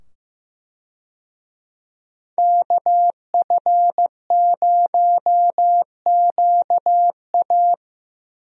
Morse code, heard as a keyed tone, about 15 wpm, 700 Hz. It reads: KF0QA
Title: USA Call Signs - 5 Characters - 15wpm